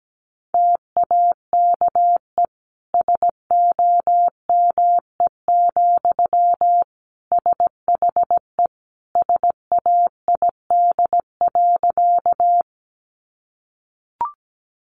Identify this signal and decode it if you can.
Morse code, heard as a keyed tone, about 17 wpm, 700 Hz. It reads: TAKE SOME, SHE SAID.
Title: Sentences from Top 200 Words 17wpm